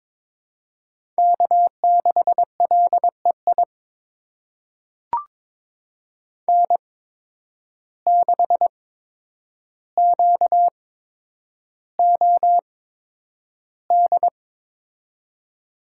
Morse code, heard as a keyed tone, about 22 wpm, spaced as for 9 wpm, 700 Hz. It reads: K6LEI N6QOD
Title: USA Call Signs - 5 Characters - 8x Character Spacing - 22wpm